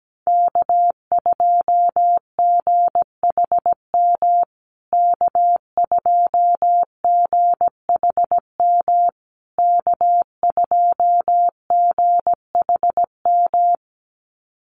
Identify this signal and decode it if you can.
Morse code, heard as a keyed tone, about 17 wpm, 700 Hz. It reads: K2GHM K2GHM K2GHM
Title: USA Call Signs Repeated 3x 17wpm